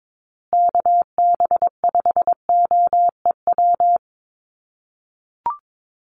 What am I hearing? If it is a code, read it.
Morse code, heard as a keyed tone, about 22 wpm, 700 Hz. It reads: KB5OEW
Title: USA Call Signs Repeated 3x 22wpm